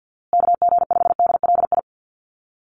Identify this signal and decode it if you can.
Morse code, heard as a keyed tone, about 50 wpm, 700 Hz. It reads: KG5DLI
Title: USA Call Signs Repeated 3x 50wpm